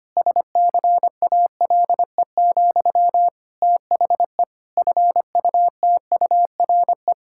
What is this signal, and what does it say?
Morse code, heard as a keyed tone, about 25 wpm, 700 Hz. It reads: SCALE, THE FUTURE
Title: Sentences from Top 1000 Words - Review Part 5 - 25wpm